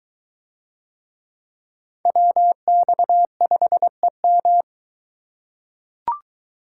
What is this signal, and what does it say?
Morse code, heard as a keyed tone, about 20 wpm, 700 Hz. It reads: WX5EM